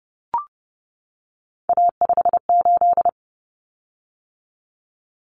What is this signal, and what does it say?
Morse code, heard as a keyed tone, about 30 wpm, 700 Hz. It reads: A58
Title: Common Cut Numbers - 3-Digit 30wpm